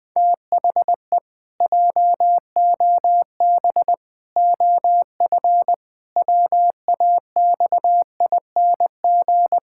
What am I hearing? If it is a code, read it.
Morse code, heard as a keyed tone, about 20 wpm, 700 Hz. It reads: THE JOB OF WAXING
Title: Pangram Challenge v2 - 20wpm